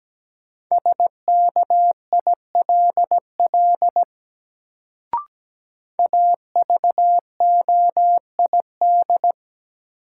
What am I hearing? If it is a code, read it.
Morse code, heard as a keyed tone, about 17 wpm, 700 Hz. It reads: SKILL AVOID
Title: Sentences from Top 1200 Words - New Word Drill Alt - 17wpm